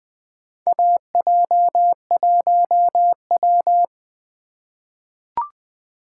Morse code, heard as a keyed tone, about 20 wpm, 700 Hz. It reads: AJ1W
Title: USA Call Signs - 4 Characters - 3x Character Spacing - 20wpm